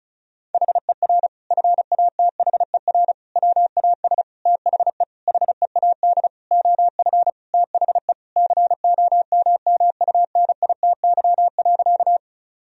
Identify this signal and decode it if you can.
Morse code, heard as a keyed tone, about 34 wpm, 700 Hz. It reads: HER FATHER WAS THE HEAD OF THE COMMUNITY.
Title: Sentences from Top 1000 Words - Review Part 4 - 35wpm